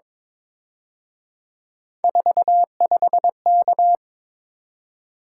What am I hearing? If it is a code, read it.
Morse code, heard as a keyed tone, about 26 wpm, 700 Hz. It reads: 45K
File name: Speed-Racing - Three Letter-Number Combinations - 22wpm